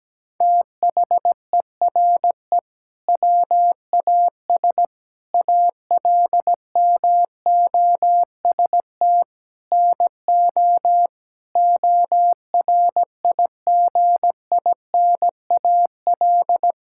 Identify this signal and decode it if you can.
Morse code, heard as a keyed tone, about 17 wpm, 700 Hz. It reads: THERE WAS ALMOST NO ORIGINAL
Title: Sentences from Top 1000 Words - Review Part 6 - 17wpm